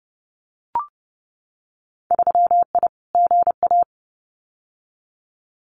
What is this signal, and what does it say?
Morse code, heard as a keyed tone, about 30 wpm, 700 Hz. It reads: 3I GA
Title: Winter Field Day Exchanges 30wpm